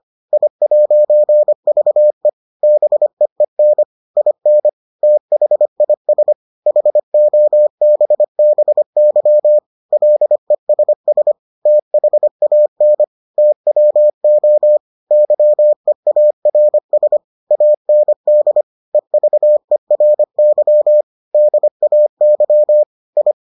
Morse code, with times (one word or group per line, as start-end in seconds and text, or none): 0.3-2.3 I'VE
2.6-3.8 BEEN
4.2-4.7 IN
5.0-6.3 THIS
6.7-9.6 HOBBY
9.9-11.3 LESS
11.7-13.0 THAN
13.4-14.8 TWO
15.1-17.2 YEARS
17.5-18.6 AND
18.9-21.0 EVERY
21.3-22.8 DAY
23.2-23.3 I